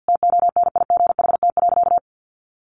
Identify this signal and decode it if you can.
Morse code, heard as a keyed tone, about 50 wpm, 700 Hz. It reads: TONIGHT.
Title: Sentences from Top 1300 Words 50wpm